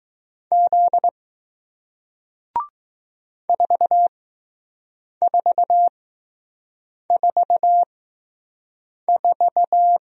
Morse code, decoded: Z 4444

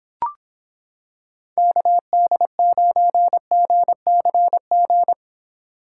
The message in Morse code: KD9GCG

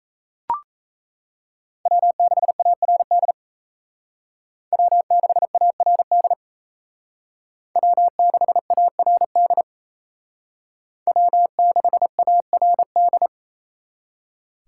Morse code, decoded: W6ARD W6ARD W6ARD W6ARD

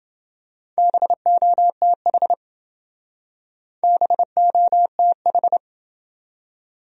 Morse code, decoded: BOTH BOTH

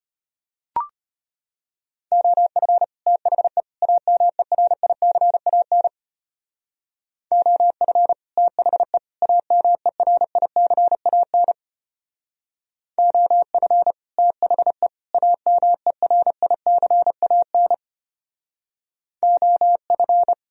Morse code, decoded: OF THE AMERICAN OF THE AMERICAN OF THE AMERICAN OF

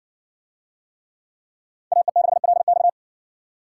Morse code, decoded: AE6LX

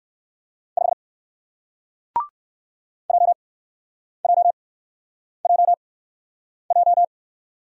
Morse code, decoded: 5 JJJJ

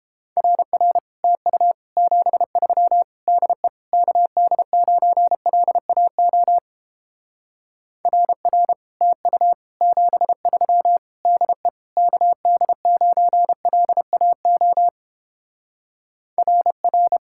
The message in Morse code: RR TU 73 DE KD9LAO RR TU 73 DE KD9LAO RR

